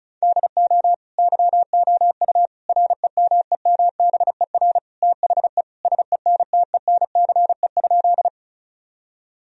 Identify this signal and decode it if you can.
Morse code, heard as a keyed tone, about 36 wpm, 700 Hz. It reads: DO YOU REMEMBER THE SENTENCE?
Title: Sentences from Top 1000 Words - Review Part 1 - 35wpm